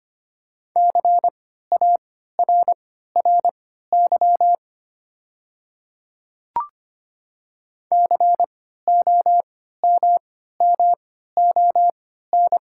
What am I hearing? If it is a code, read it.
Morse code, heard as a keyed tone, about 25 wpm, spaced as for 14 wpm, 690 Hz. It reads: CARRY COMMON
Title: 3x Character Spacing - Top 500 Words - 25wpm